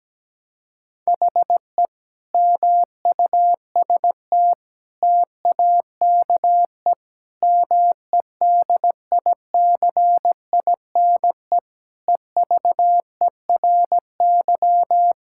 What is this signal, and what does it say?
Morse code, heard as a keyed tone, about 17 wpm, 700 Hz. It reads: HE MUST TAKE MEDICINE EVERY
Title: Sentences from Top 1000 Words - Review Part 6 - 17wpm